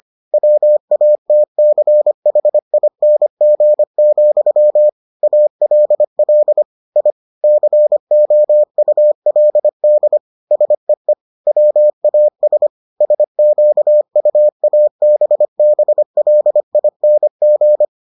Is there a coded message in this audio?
Yes: Morse code from 0.3 to 17.9 s, heard as a keyed tone, about 25 wpm, 600 Hz. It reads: WATCHING, ALL I COULD SEE WAS SQUABBLING